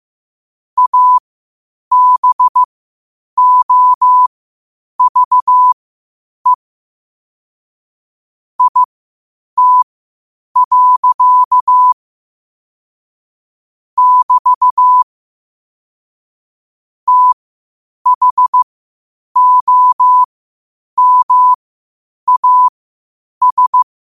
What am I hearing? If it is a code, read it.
Morse code, heard as a keyed tone, about 15 wpm, spaced as for 8 wpm, 1000 Hz. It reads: ABOVE IT. <BT> THOMAS
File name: qotdfiles / 8wpm